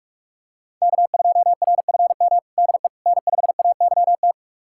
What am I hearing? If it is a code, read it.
Morse code, heard as a keyed tone, about 44 wpm, 700 Hz. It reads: K2RFM DE N5AYT
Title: USA Callsign DE Callsign - 5 Characters - 45wpm